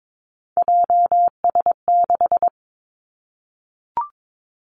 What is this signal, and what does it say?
Morse code, heard as a keyed tone, about 22 wpm, 700 Hz. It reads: JS6
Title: Three Challenging Letter-Number Combinations 22wpm